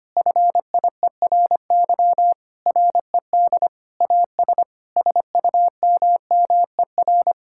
Morse code, heard as a keyed tone, about 25 wpm, 700 Hz. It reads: FIERY RED AS SUMMER
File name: practice / 25wpm / Difficult Sentences with Hints - 25wpm